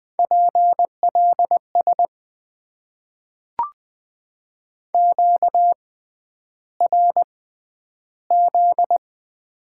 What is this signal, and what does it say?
Morse code, heard as a keyed tone, about 20 wpm, spaced as for 10 wpm, 700 Hz. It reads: PLS QRZ
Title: Top 90 QSO Elements - 6x Character Spacing - 20wpm